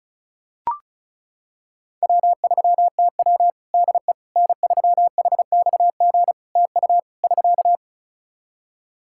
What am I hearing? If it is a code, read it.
Morse code, heard as a keyed tone, about 34 wpm, 700 Hz. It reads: W3TW DE N3HXG TU <SK>
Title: Practice QSO - Exchange 4 - Ending - 35wpm